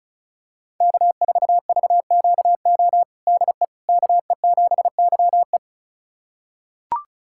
Morse code, heard as a keyed tone, about 36 wpm, 700 Hz. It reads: K4VQO DE KE7YE
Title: USA Callsign DE Callsign - 5 Characters - 35wpm